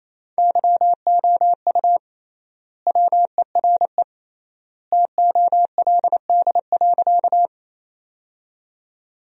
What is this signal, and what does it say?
Morse code, heard as a keyed tone, about 28 wpm, 700 Hz. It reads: YOU WERE TOLD.
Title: 3x Word-Spacing - Sentences from Top 400 Words - 28wpm